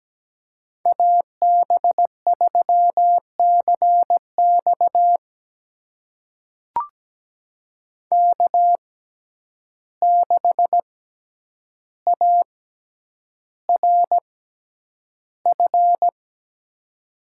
Morse code, heard as a keyed tone, about 17 wpm, spaced as for 9 wpm, 700 Hz. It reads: AB3CX K6ARF
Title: USA Call Signs - 5 Characters - 6x Character Spacing - 17wpm